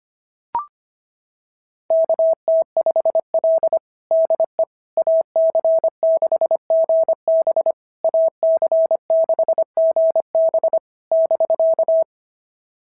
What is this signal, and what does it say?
Morse code, heard as a keyed tone, about 23 wpm, 650 Hz. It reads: KT5L DE AC6GB AC6GB <BK>
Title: SOTA QSO Practice 25wpm